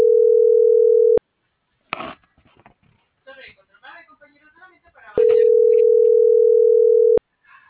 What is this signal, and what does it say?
Ringback tone (North America), 440 + 480 Hz.